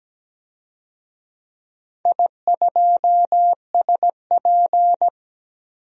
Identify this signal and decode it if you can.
Morse code, heard as a keyed tone, about 17 wpm, 700 Hz. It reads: I2SP
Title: DXCC-100 Call Signs - 4 Characters - 17wpm